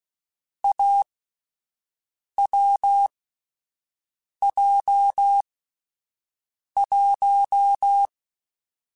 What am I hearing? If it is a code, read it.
Morse code, heard as a keyed tone, about 16 wpm, spaced as for 5 wpm, 800 Hz. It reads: AWJ1
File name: Difficult Combos / Difficult Combos-mp3s / awj1 ndb6 gz7 @5